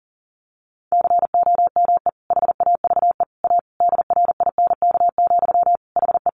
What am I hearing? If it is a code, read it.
Morse code, heard as a keyed tone, about 37 wpm, 700 Hz. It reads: COME HAVE A DRINK, HE
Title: Sentences from Top 1500 Words 40wpm